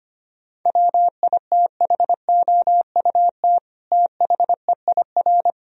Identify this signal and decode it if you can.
Morse code, heard as a keyed tone, about 25 wpm, 700 Hz. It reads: WITHOUT THEIR
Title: Sentences from Top 1000 Words - Review Part 3 - 25wpm